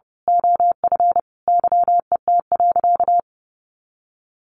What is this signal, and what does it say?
Morse code, heard as a keyed tone, about 30 wpm, 700 Hz. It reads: OF YET.
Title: A Sound Of Thunder 30wpm